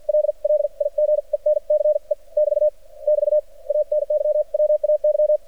Morse code, heard as a keyed tone, about 53 wpm, spaced as for 42 wpm, 600 Hz. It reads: LLIGENCE <BT> <BT> ANYWAY
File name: archived / CRSnet